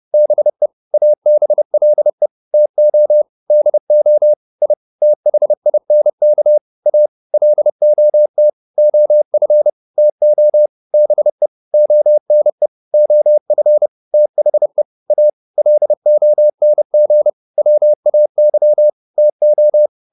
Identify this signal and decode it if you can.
Morse code, heard as a keyed tone, about 30 wpm, 600 Hz. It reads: BE ABLE TO DO I THINK A LOT OF TO BE ONE OF THE A LONG WAY TO